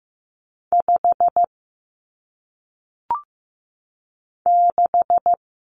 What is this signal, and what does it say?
Morse code, heard as a keyed tone, about 15 wpm, 700 Hz. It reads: 5 6
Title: Numbers - 1-Digit 15wpm